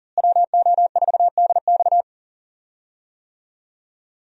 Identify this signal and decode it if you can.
Morse code, heard as a keyed tone, about 37 wpm, 700 Hz. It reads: WO4DX
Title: USA Call Signs - 5 Characters - Repeated 2x - 40wpm